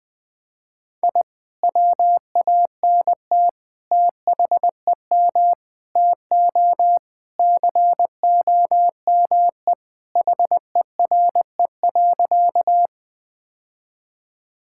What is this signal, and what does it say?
Morse code, heard as a keyed tone, about 20 wpm, 700 Hz. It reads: I WANT THEM TO COME HERE.
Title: Sentences with 2-Word Phrases - 20wpm